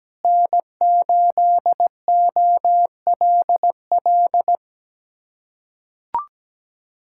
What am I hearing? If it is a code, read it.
Morse code, heard as a keyed tone, about 17 wpm, 700 Hz. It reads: N8OLL
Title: USA Callsign DE Callsign 17wpm